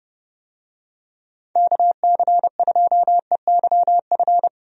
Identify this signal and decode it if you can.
Morse code, heard as a keyed tone, about 30 wpm, 700 Hz. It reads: KC2EYF
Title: USA Call Signs - 6 Characters - 30wpm